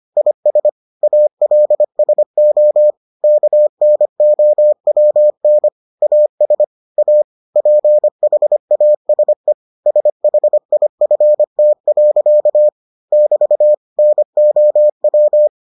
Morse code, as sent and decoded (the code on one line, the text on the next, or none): .. ... / .- .-.. ... --- / -.- -. --- .-- -. / .- ... / .- / .--. .... .- ... / ... .... .. ..-. - .-.-.- / -...- / -. --- .--
IS ALSO KNOWN AS A PHASE SHIFT. <BT> NOW